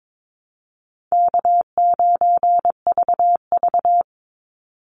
Morse code, heard as a keyed tone, about 22 wpm, 700 Hz. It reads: K9VV
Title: Easy Call Signs Repeated 3x - 22wpm